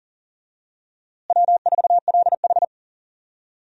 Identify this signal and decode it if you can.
Morse code, heard as a keyed tone, about 38 wpm, 700 Hz. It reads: W4LH
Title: USA Call Signs - 4 Characters - Repeated 2x - 40wpm